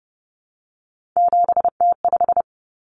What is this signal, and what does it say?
Morse code, heard as a keyed tone, about 30 wpm, 700 Hz. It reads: 7T5
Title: Common Cut Numbers - 3-Digit 30wpm